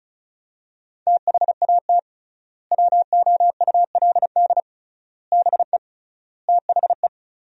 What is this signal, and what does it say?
Morse code, heard as a keyed tone, about 34 wpm, 700 Hz. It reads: THAT WOULD BE THE
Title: 3x Word-Spacing - Sentences from Top 400 Words - 35wpm